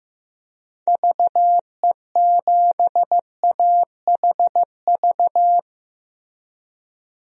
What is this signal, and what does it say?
Morse code, heard as a keyed tone, about 15 wpm, 700 Hz. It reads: VE7AHV
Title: DXCC-100 Call Signs - 6 Characters - Repeated 2x - 15wpm